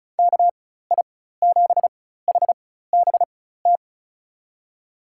Morse code, heard as a keyed tone, about 36 wpm, spaced as for 16 wpm, 700 Hz. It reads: KI7HBT
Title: USA Call Signs - 6 Characters - 4x Character Spacing - 35wpm